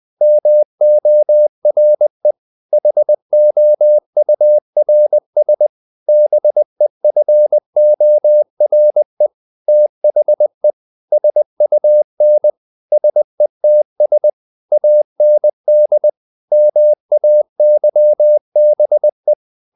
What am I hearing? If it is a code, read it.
Morse code, heard as a keyed tone, about 20 wpm, 600 Hz. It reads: MORE HOURS BEFORE THE SUN SETS AND MAYBE
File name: the old man and the sea / the old man and the sea part 1 20wpm